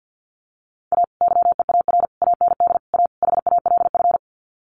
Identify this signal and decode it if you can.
Morse code, heard as a keyed tone, about 50 wpm, 700 Hz. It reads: A YEAR AND A HALF
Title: Sets of 5 Words 50wpm